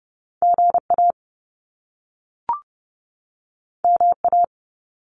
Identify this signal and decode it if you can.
Morse code, heard as a keyed tone, about 30 wpm, 700 Hz. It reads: GA MA
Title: US States 30wpm